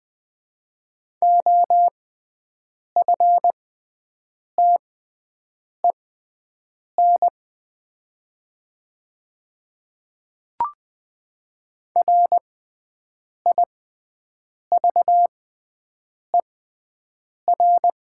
Morse code, heard as a keyed tone, about 20 wpm, spaced as for 7 wpm, 700 Hz. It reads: OFTEN RIVER
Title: 6x Character Spacing - Top 300 Words - 20wpm